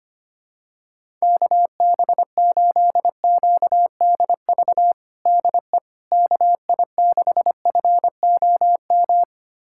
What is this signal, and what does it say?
Morse code, heard as a keyed tone, about 25 wpm, 700 Hz. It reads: KB8QDV DE KI6FOM
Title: USA Callsign DE Callsign - 6 Characters - 25wpm